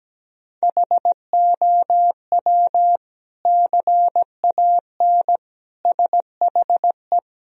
Morse code, heard as a keyed tone, about 17 wpm, 700 Hz. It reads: HOW CAN SHE
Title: Sentences from Top 1000 Words - Review Part 4 - 17wpm